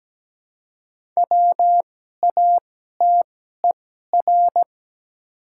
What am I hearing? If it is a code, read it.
Morse code, heard as a keyed tone, about 17 wpm, spaced as for 12 wpm, 700 Hz. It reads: WATER
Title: 2x Character Spacing - Top 100 Words - 17wpm